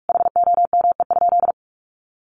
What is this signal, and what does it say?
Morse code, heard as a keyed tone, about 44 wpm, 700 Hz. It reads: HOME?
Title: Sentences from Top 1100 Words 45wpm